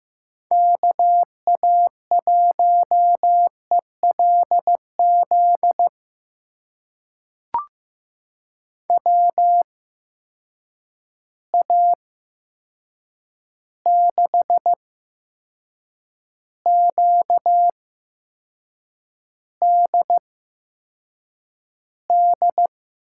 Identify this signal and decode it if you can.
Morse code, heard as a keyed tone, about 15 wpm, spaced as for 6 wpm, 700 Hz. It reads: KA1ELZ WA6QDD
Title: USA Call Signs - 6 Characters - 8x Character Spacing - 15wpm